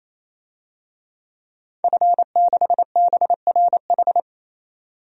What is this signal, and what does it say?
Morse code, heard as a keyed tone, about 28 wpm, 700 Hz. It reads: F6BRH
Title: DXCC-100 Call Signs - 5 Characters - 28wpm